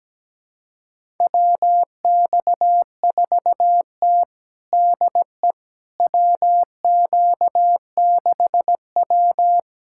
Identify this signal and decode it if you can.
Morse code, heard as a keyed tone, about 17 wpm, 700 Hz. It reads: WX4T DE WQ6W
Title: USA Callsign DE Callsign - 4 Characters - 17wpm